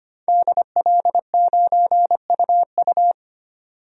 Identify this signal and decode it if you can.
Morse code, heard as a keyed tone, about 25 wpm, 700 Hz. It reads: DL9UU
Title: DXCC-100 Call Signs - 5 Characters - 25wpm